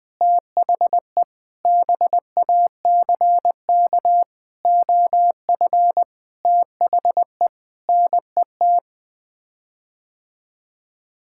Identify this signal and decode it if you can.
Morse code, heard as a keyed tone, about 21 wpm, 700 Hz. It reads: THE BACK OF THE NET
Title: Sets of 8 Words - 20wpm